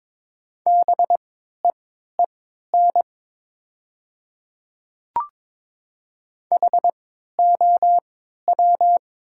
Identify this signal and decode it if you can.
Morse code, heard as a keyed tone, about 22 wpm, spaced as for 13 wpm, 700 Hz. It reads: BEEN HOW
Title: 3x Character Spacing - Top 300 Words - 22wpm